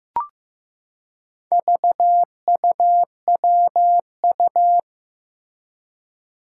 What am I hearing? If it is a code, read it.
Morse code, heard as a keyed tone, about 15 wpm, 700 Hz. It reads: VUWU
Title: Four Challenging Letter-Number Combinations 15wpm